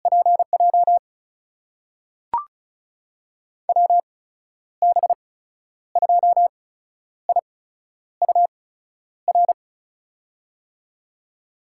Morse code, heard as a keyed tone, about 34 wpm, spaced as for 11 wpm, 700 Hz. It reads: PJ WB2IUR
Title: USA Call Signs - 6 Characters - 8x Character Spacing - 35wpm